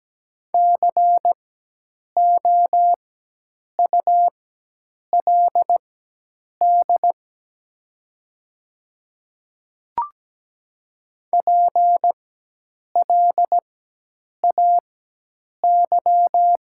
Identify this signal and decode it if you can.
Morse code, heard as a keyed tone, about 17 wpm, spaced as for 8 wpm, 700 Hz. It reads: COULD PLAY
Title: 4x Character Spacing - Top 200 Words - 17wpm